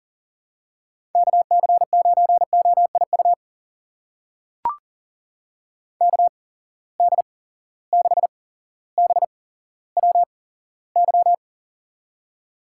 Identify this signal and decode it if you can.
Morse code, heard as a keyed tone, about 42 wpm, spaced as for 17 wpm, 700 Hz. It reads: KC9OIU KD6BWY
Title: USA Call Signs - 6 Characters - 8x Character Spacing - 40wpm